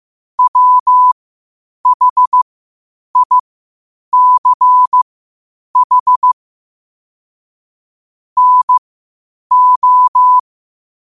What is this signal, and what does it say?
Morse code, heard as a keyed tone, about 15 wpm, spaced as for 8 wpm, 1000 Hz. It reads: WHICH NO
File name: qotdfiles / 8wpm